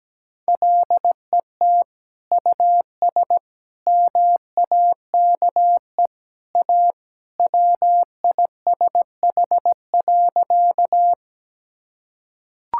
Morse code, heard as a keyed tone, about 17 wpm, 700 Hz. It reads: LET US MAKE A WISH.
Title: Sentences from Top 1600 Words 17wpm